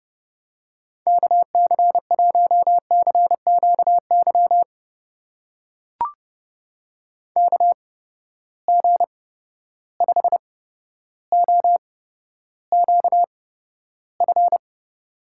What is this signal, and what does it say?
Morse code, heard as a keyed tone, about 30 wpm, spaced as for 13 wpm, 700 Hz. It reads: KC1CQY KG5OQF
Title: USA Call Signs - 6 Characters - 8x Character Spacing - 30wpm